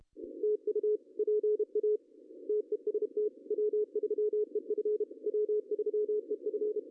Morse code, heard as a keyed tone, about 33 wpm, 410 Hz. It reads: TU PA TEST W3EF W3EF
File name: CW / cqww-cw-k5tr